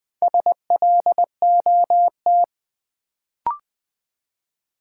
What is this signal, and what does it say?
Morse code, heard as a keyed tone, about 20 wpm, 700 Hz. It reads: SLOT
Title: Build A Word - 2 Letters At A Time - 4-Letter Words - 20wpm